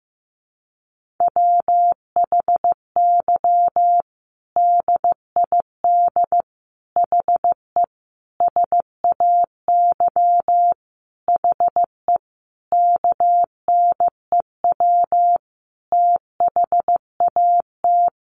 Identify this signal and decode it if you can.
Morse code, heard as a keyed tone, about 15 wpm, 700 Hz. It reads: WHY DID HE SAY HE KNEW THAT
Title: Sentences from Top 1500 Words 15wpm